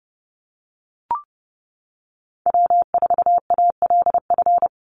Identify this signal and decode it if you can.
Morse code, heard as a keyed tone, about 30 wpm, 700 Hz. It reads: W4ALF
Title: Call Signs - 30wpm